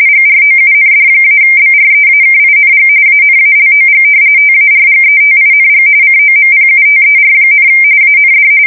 RTTY, about 45.45 baud. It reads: GROUND-BREAKING ANALOG TARGETING COMPUTER, AND
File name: fsk170 / battleshipnj